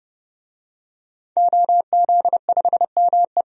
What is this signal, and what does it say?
Morse code, heard as a keyed tone, about 30 wpm, 700 Hz. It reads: OZ5ME
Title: DXCC-100 Call Signs - 5 Characters - 30wpm